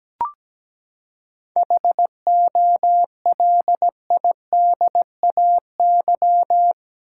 Morse code, heard as a keyed tone, about 17 wpm, 700 Hz. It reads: HOLIDAY
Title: Seven Letter Words - 17wpm